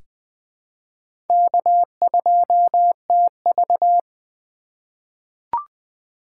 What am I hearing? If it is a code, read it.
Morse code, heard as a keyed tone, about 20 wpm, 700 Hz. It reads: K2TV
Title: K1USN SST QSO Practice - 20wpm